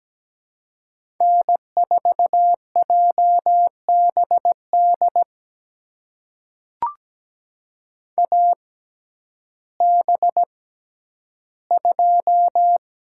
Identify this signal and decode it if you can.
Morse code, heard as a keyed tone, about 17 wpm, spaced as for 10 wpm, 700 Hz. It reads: N4JBD AB2
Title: USA Call Signs - 5 Characters - 6x Character Spacing - 17wpm